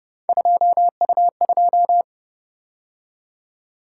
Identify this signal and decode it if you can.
Morse code, heard as a keyed tone, about 29 wpm, 700 Hz. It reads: 2U2